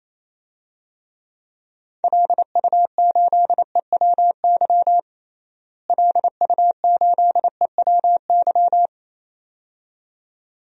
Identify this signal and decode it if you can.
Morse code, heard as a keyed tone, about 28 wpm, 700 Hz. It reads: LU8EWY LU8EWY
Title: DXCC-100 Call Signs - 6 Characters - Repeated 2x - 28wpm